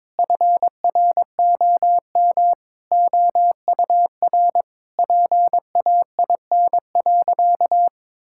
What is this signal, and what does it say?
Morse code, heard as a keyed tone, about 22 wpm, 700 Hz. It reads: FROM OUR PAIN.
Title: Sentences from Top 1600 Words 22wpm